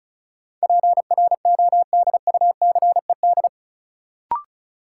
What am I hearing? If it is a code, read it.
Morse code, heard as a keyed tone, about 34 wpm, 700 Hz. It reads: PRODUCED
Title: Sentences from Top 1000 Words - Review Part 6 - New Word Drill - 35wpm